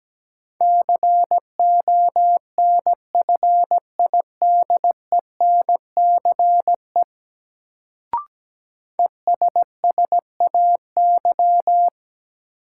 Morse code, heard as a keyed tone, about 17 wpm, 700 Hz. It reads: CONFIDENCE ESSAY
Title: Sentences from Top 1500 Words - New Word Drill Alt - 17wpm